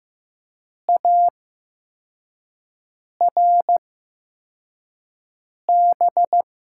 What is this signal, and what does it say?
Morse code, heard as a keyed tone, about 15 wpm, spaced as for 4 wpm, 700 Hz. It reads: ARB
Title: USA Call Signs - 8x Character Spacing - 15wpm